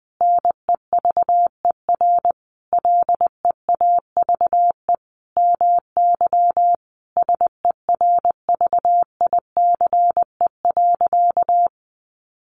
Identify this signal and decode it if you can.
Morse code, heard as a keyed tone, about 20 wpm, 700 Hz. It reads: NEVER LEAVE MY SERVICE.
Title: Sentences from Top 700 Words 20wpm